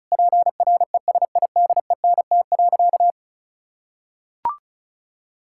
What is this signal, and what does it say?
Morse code, heard as a keyed tone, about 36 wpm, 690 Hz. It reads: PRESIDENT.
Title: Sentences from Top 1600 Words 35wpm